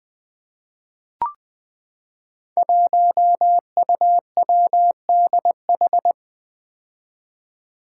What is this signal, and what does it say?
Morse code, heard as a keyed tone, about 20 wpm, 700 Hz. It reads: 1UWDH
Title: Five Challenging Letter-Number Combinations 20wpm